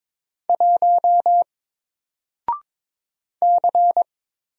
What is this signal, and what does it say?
Morse code, heard as a keyed tone, about 22 wpm, 700 Hz. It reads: 1 C
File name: Single-Letter-Number-22wpm